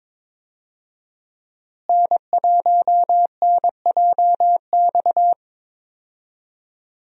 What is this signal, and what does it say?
Morse code, heard as a keyed tone, about 22 wpm, 700 Hz. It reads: N1NJX
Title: USA Call Signs - 5 Characters - Repeated 2x - 22wpm